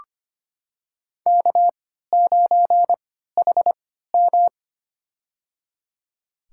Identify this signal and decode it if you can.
Morse code, heard as a keyed tone, about 25 wpm, spaced as for 14 wpm, 700 Hz. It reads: K9HM
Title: USA Call Signs - 3x Character Spacing - 25wpm